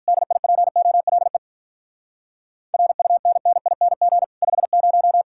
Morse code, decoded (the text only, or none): DIPOLE RUNNING 50